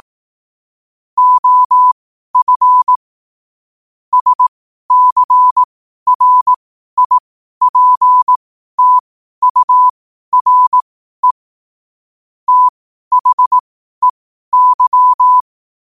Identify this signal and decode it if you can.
Morse code, heard as a keyed tone, about 18 wpm, spaced as for 12 wpm, 1000 Hz. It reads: OF SCRIPTURE THEY
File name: qotdfiles / 12wpm